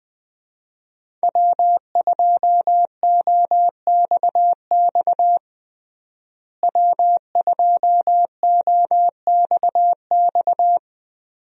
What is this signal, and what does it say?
Morse code, heard as a keyed tone, about 20 wpm, 700 Hz. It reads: W2OXX W2OXX
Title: USA Call Signs - 5 Characters - Repeated 2x - 20wpm